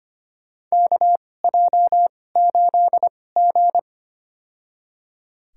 Morse code, heard as a keyed tone, about 25 wpm, spaced as for 18 wpm, 700 Hz. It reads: KJ8G
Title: USA Call Signs - 4 Characters - 2x Character Spacing - 25wpm